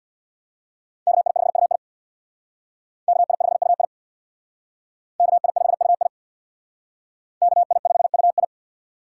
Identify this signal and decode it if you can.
Morse code, heard as a keyed tone, about 63 wpm, 700 Hz. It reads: KI5UI KI5UI KI5UI KI5UI